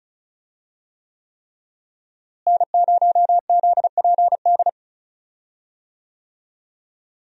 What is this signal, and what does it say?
Morse code, heard as a keyed tone, about 36 wpm, 700 Hz. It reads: N0ZPD
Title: USA Call Signs - 5 Characters - Repeated 3x - 35wpm